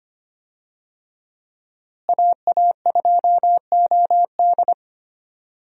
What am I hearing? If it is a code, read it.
Morse code, heard as a keyed tone, about 25 wpm, 700 Hz. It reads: AA2OD
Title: USA Call Signs - 5 Characters - Repeated 3x - 25wpm